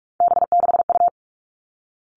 Morse code, heard as a keyed tone, about 46 wpm, 700 Hz. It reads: B6U
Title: Three Challenging Letter-Number Combinations 45wpm